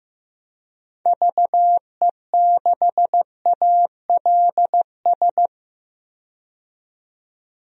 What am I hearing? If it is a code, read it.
Morse code, heard as a keyed tone, about 15 wpm, 700 Hz. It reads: VE6ALS
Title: DXCC-100 Call Signs - 6 Characters - 15wpm